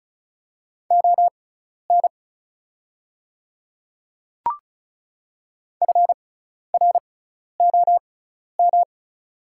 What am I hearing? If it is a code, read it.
Morse code, heard as a keyed tone, about 36 wpm, spaced as for 12 wpm, 700 Hz. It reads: ON FROM
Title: 6x Character Spacing - Top 200 Words - 35wpm